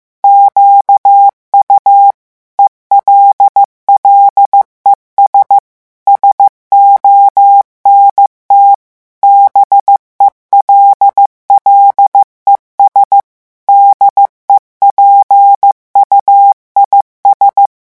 Morse code, heard as a keyed tone, about 15 wpm, 800 Hz. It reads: QU ELLES SONT BELLES DEPUIS
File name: morse mp3 / revision9 citations humoristiques vitesse 14 mots minute